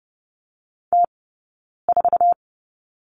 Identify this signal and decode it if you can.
Morse code, heard as a keyed tone, about 30 wpm, spaced as for 9 wpm, 690 Hz. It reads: T4
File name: Lesson 19d - Letter w - Two Characters 30wpm